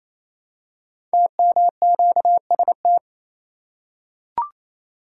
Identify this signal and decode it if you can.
Morse code, heard as a keyed tone, about 29 wpm, 700 Hz. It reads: TMQST
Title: Code Groups - 5 Letter:Numbers - 28wpm